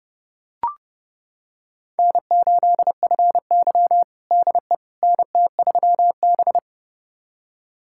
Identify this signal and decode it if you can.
Morse code, heard as a keyed tone, about 30 wpm, 700 Hz. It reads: N8FY DE NT3B
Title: USA Callsign DE Callsign - 4 Characters - 30wpm